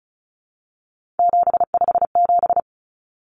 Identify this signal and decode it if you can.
Morse code, heard as a keyed tone, about 34 wpm, 700 Hz. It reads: 757
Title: Common Cut Numbers - 3-Digit 35wpm